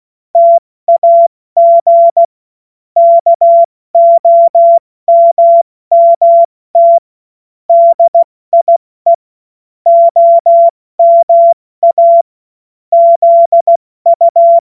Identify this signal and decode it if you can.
Morse code, heard as a keyed tone, about 16 wpm, 680 Hz. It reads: TAG KOMMT DIE OMA ZU
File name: Morsekurs / Witze / Witz 28. September 2021, 70 F80